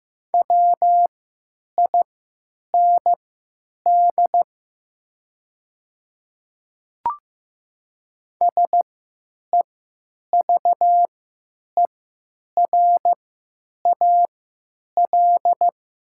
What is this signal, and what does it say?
Morse code, heard as a keyed tone, about 15 wpm, spaced as for 9 wpm, 700 Hz. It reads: WIND SEVERAL
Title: 3x Character Spacing - Top 500 Words - 15wpm